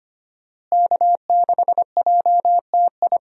Morse code, heard as a keyed tone, about 25 wpm, 700 Hz. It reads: K6JTI
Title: USA Call Signs - 5 Characters - Repeated 3x - 25wpm